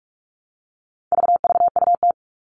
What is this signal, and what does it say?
Morse code, heard as a keyed tone, about 44 wpm, 700 Hz. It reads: VVUT
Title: All Cut Numbers - 4-Digit 45wpm